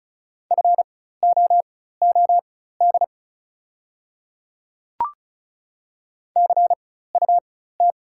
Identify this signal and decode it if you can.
Morse code, heard as a keyed tone, about 35 wpm, spaced as for 16 wpm, 700 Hz. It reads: FOOD CUT
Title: 4x Character Spacing - Top 300 Words - 35wpm